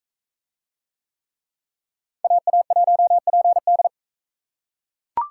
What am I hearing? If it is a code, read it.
Morse code, heard as a keyed tone, about 44 wpm, 700 Hz. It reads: AA1PD